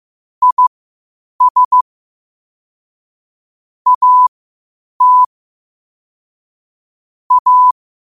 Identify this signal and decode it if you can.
Morse code, heard as a keyed tone, about 15 wpm, spaced as for 8 wpm, 1000 Hz. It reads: IS AT A
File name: qotdfiles / 8wpm